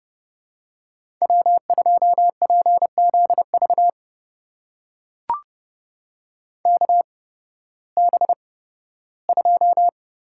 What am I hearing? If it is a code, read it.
Morse code, heard as a keyed tone, about 30 wpm, spaced as for 16 wpm, 700 Hz. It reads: W2PZV KB2
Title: USA Call Signs - 8x Character Spacing - 30wpm